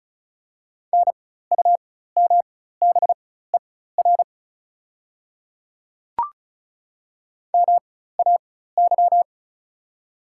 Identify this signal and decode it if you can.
Morse code, heard as a keyed tone, about 36 wpm, spaced as for 16 wpm, 700 Hz. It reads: NUMBER MAY
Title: 4x Character Spacing - Top 100 Words - 35wpm